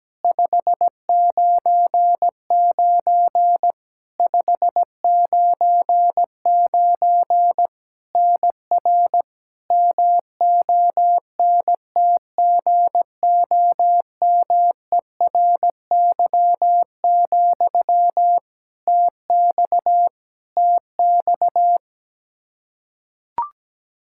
Morse code, heard as a keyed tone, about 17 wpm, 700 Hz. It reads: 599 599 NR MONTGOMERY, TX TX
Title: Practice QSO - Exchange 1 - Beginning - 17wpm